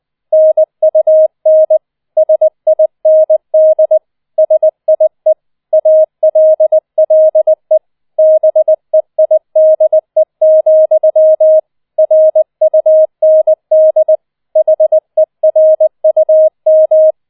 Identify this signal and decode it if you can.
Morse code, heard as a keyed tone, about 20 wpm, 630 Hz. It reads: NUN SIND SIE ALLE BEIDE, RUND HERUM